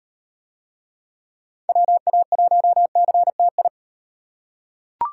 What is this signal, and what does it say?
Morse code, heard as a keyed tone, about 36 wpm, 700 Hz. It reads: WA1CTI